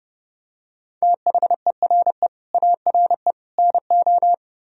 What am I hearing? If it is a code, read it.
Morse code, heard as a keyed tone, about 29 wpm, 700 Hz. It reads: THERE ARE NO
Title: Speed-Racing - Sets of 3 Words 30wpm